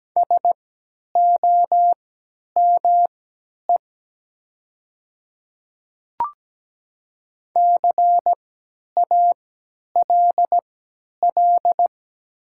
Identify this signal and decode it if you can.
Morse code, heard as a keyed tone, about 17 wpm, spaced as for 10 wpm, 700 Hz. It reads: SOME CALL
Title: 3x Character Spacing - Top 200 Words - 17wpm